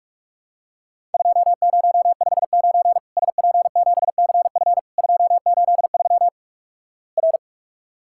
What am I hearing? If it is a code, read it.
Morse code, heard as a keyed tone, about 47 wpm, 700 Hz. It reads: 2059 SP7CF 283 R
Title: Worked All Europe DX Contest - QTC Exchange - 45wpm